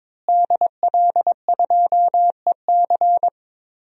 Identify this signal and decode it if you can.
Morse code, heard as a keyed tone, about 22 wpm, 700 Hz. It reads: DL2EC
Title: DXCC-100 Call Signs - 5 Characters - Repeated 3x - 22wpm